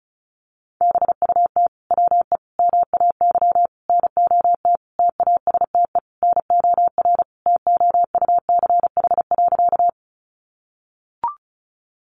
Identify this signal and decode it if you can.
Morse code, heard as a keyed tone, about 34 wpm, 700 Hz. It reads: BUT WE MAY NOT TASTE NOR TOUCH.
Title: Sentences from Top 1500 Words 35wpm